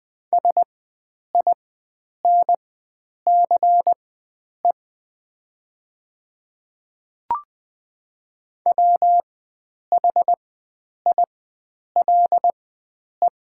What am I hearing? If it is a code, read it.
Morse code, heard as a keyed tone, about 20 wpm, spaced as for 9 wpm, 700 Hz. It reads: SINCE WHILE
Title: 4x Character Spacing - Top 500 Words - 20wpm